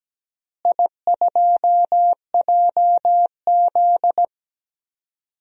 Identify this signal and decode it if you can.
Morse code, heard as a keyed tone, about 17 wpm, 700 Hz. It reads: I2JZ
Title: DXCC-100 Call Signs - 4 Characters - Repeated 3x - 17wpm